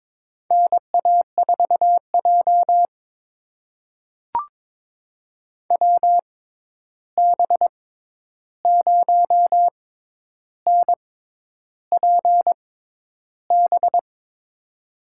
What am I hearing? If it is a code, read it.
Morse code, heard as a keyed tone, about 22 wpm, spaced as for 10 wpm, 700 Hz. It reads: NA4J WB0NPB
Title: USA Call Signs - 6x Character Spacing - 22wpm